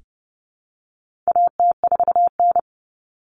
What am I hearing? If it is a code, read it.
Morse code, heard as a keyed tone, about 30 wpm, 700 Hz. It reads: AT4N
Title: Common Cut Numbers - 4-Digit 30wpm